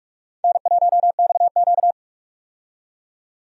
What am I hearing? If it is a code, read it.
Morse code, heard as a keyed tone, about 46 wpm, 700 Hz. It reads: N1XQ
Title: USA Call Signs - 4 Characters - 45wpm